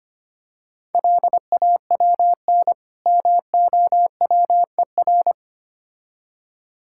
Morse code, heard as a keyed tone, about 25 wpm, 700 Hz. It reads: LAWN MOWER
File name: Easy Words - Speed-Racing with Hints - 25wpm